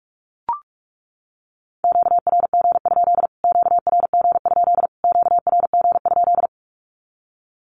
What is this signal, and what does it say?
Morse code, heard as a keyed tone, about 44 wpm, 700 Hz. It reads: QRG? QRG? QRG?
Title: Q-Codes Repeated 3x 45wpm